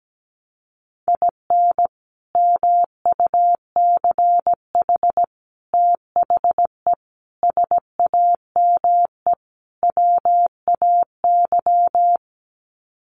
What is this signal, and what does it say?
Morse code, heard as a keyed tone, about 17 wpm, 700 Hz. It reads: IN MUCH THE SAME WAY
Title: Sets of 5 Words Encore 17wpm